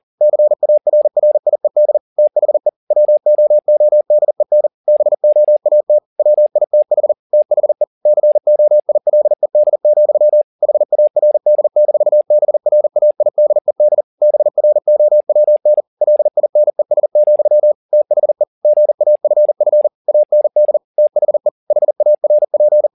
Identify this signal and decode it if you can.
Morse code, heard as a keyed tone, about 40 wpm, 600 Hz. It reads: CARRIED THE WOODEN BOAT WITH THE COILED, HARD-BRAIDED BROWN LINES, THE GAFF AND THE HARP
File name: the old man and the sea 1 40wpm